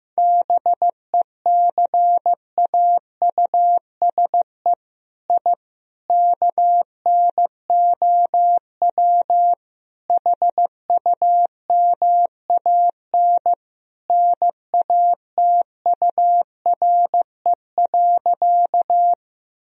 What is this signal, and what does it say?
Morse code, heard as a keyed tone, about 16 wpm, 700 Hz. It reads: BECAUSE I KNOW HUMAN NATURE.
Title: Sentences from Top 1000 Words - Review Part 3 - 15wpm